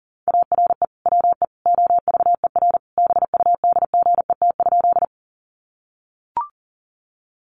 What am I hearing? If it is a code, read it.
Morse code, heard as a keyed tone, about 40 wpm, 700 Hz. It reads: ARE WE OVER BUDGET?
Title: Sentences from Top 1200 Words 40wpm